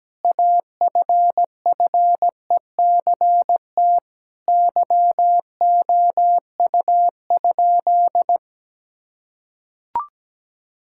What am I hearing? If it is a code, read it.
Morse code, heard as a keyed tone, about 17 wpm, 700 Hz. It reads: AFFECT YOU?
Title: Sentences from Top 1000 Words - Review Part 6 - 17wpm